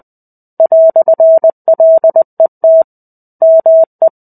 Morse code, heard as a keyed tone, about 20 wpm, 650 Hz. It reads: "LET ME
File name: 05-Chapter-05 20 WPM